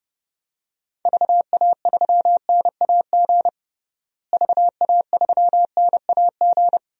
Morse code, heard as a keyed tone, about 32 wpm, 700 Hz. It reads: VA3NAG VA3NAG
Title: DXCC-100 Call Signs - 6 Characters - Repeated 3x - 30wpm